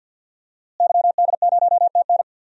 Morse code, heard as a keyed tone, about 47 wpm, 700 Hz. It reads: YD0TN
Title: DXCC-100 Call Signs - 5 Characters - Repeated 3x - 50wpm